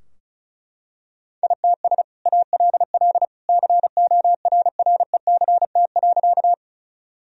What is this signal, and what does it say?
Morse code, heard as a keyed tone, about 36 wpm, 700 Hz. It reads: ITS ALL CORRECT.
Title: Sentences from Top 1000 Words - Review Part 2 - 35wpm